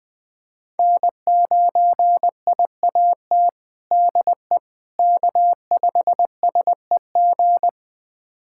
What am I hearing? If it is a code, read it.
Morse code, heard as a keyed tone, about 20 wpm, 700 Hz. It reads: N9IAT DE K5SEG
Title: USA Callsign DE Callsign - 5 Characters - 20wpm